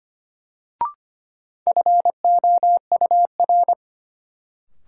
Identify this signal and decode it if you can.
Morse code, heard as a keyed tone, about 25 wpm, 700 Hz. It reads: FOUR
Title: Sentences from Top 300 Words - New Word Drill - 25wpm